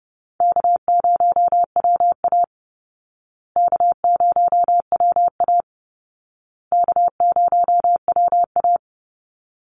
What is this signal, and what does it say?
Morse code, heard as a keyed tone, about 30 wpm, 700 Hz. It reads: K0WA K0WA K0WA
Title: Easy Call Signs Repeated 3x - 30wpm